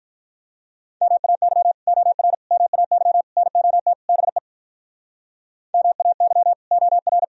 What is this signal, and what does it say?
Morse code, heard as a keyed tone, about 46 wpm, 700 Hz. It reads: MAY OR MAY NOT BE MAY OR